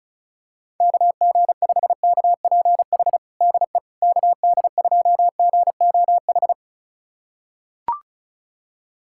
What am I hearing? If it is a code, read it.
Morse code, heard as a keyed tone, about 36 wpm, 700 Hz. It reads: KG5KPH DE KD2GOH
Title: USA Callsign DE Callsign 35wpm